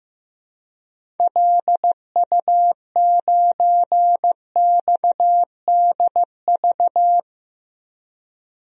Morse code, heard as a keyed tone, about 16 wpm, 700 Hz. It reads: LU9XDV
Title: DXCC-100 Call Signs - 6 Characters - Repeated 3x - 15wpm